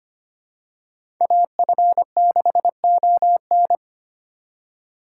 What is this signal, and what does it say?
Morse code, heard as a keyed tone, about 25 wpm, 700 Hz. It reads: AF6ON
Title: USA Call Signs - 5 Characters - 4x Character Spacing - 25wpm